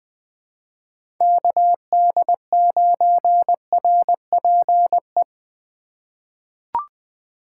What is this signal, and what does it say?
Morse code, heard as a keyed tone, about 21 wpm, 700 Hz. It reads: KD9RPE
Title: 2023 POTA Hunter Call Signs - 20wpm